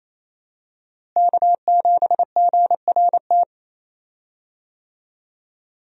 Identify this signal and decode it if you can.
Morse code, heard as a keyed tone, about 28 wpm, 700 Hz. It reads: K7GRT
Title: USA Call Signs - 5 Characters - Repeated 3x - 28wpm